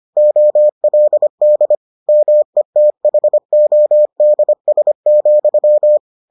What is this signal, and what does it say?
Morse code, heard as a keyed tone, about 25 wpm, 600 Hz. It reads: OLD METHODS,